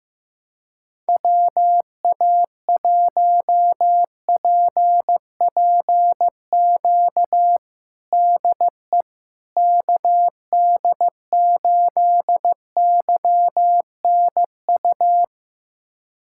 Morse code, heard as a keyed tone, about 15 wpm, 700 Hz. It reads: WA1PPQ DE KD8YNU
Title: USA Callsign DE Callsign - 6 Characters - 15wpm